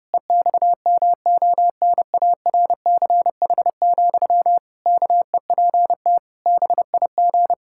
Morse code, heard as a keyed tone, about 30 wpm, 700 Hz. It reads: EXMONARCH, KEPT BIG
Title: Pangram Challenge v2 - 30wpm